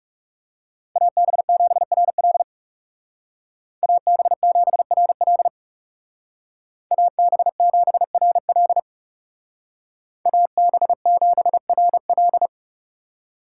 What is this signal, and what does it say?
Morse code, heard as a keyed tone, about 37 wpm, 700 Hz. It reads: AB7RL AB7RL AB7RL AB7RL
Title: Speed Racing - USA Call Signs 30wpm